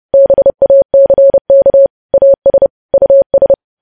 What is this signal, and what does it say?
Morse code, heard as a keyed tone, about 30 wpm, 550 Hz. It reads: BACK AS US
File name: podcast / News headlines-1x-30x30